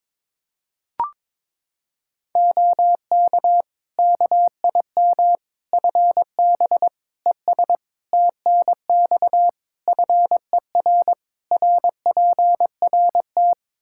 Morse code, heard as a keyed tone, about 22 wpm, 700 Hz. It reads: OK KIM FB ES TNX FER RPRT
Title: Practice QSO - Exchange 2 - Info - 22wpm